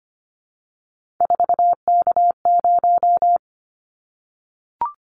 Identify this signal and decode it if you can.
Morse code, heard as a keyed tone, about 25 wpm, 700 Hz. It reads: 4K0
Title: Callsign Prefixes for Top 100 DXCC - 25wpm